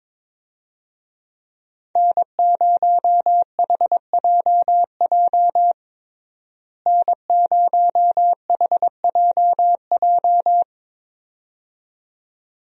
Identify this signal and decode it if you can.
Morse code, heard as a keyed tone, about 22 wpm, 700 Hz. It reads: N0HJJ N0HJJ
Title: USA Call Signs - 5 Characters - Repeated 2x - 22wpm